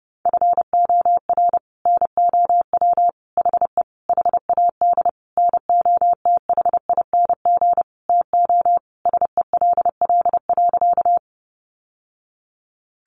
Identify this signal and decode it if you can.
Morse code, heard as a keyed tone, about 30 wpm, 700 Hz. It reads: FOR NOW HE HAD NOTHING TO SELL.
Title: Sentences from Top 1100 Words 30wpm